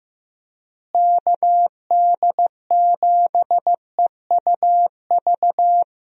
Morse code, heard as a keyed tone, about 16 wpm, 700 Hz. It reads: KD7EUV
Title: USA Call Signs - 6 Characters - Repeated 3x - 15wpm